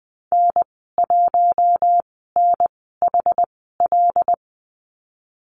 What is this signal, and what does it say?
Morse code, heard as a keyed tone, about 20 wpm, spaced as for 15 wpm, 700 Hz. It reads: N1NHL
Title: 128 Lesson 21g - Number 6 - Call Signs 20wpm